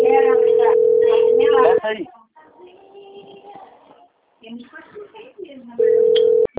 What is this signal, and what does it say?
Ringback tone (North America), 440 + 480 Hz, with speech.
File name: out- - - - - .17467